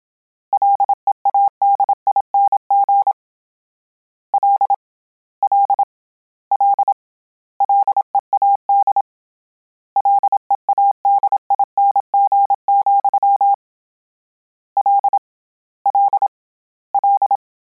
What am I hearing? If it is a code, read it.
Morse code, heard as a keyed tone, about 27 wpm, 800 Hz. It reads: LEADING L L L LEAD LEADING, L L L